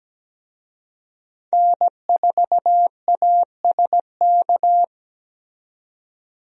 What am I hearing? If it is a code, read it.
Morse code, heard as a keyed tone, about 17 wpm, 700 Hz. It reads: N4ASK
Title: USA Call Signs - 5 Characters - 17wpm